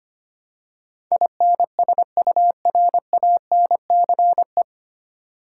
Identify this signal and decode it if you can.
Morse code, heard as a keyed tone, about 25 wpm, 700 Hz. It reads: INSURANCE
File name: Nine Letter Words - 25wpm